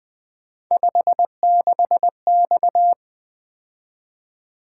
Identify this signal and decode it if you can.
Morse code, heard as a keyed tone, about 20 wpm, 700 Hz. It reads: 56X
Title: Three Letter-Number Combinations - 20wpm